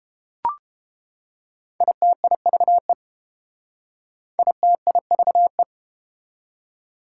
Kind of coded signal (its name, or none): Morse code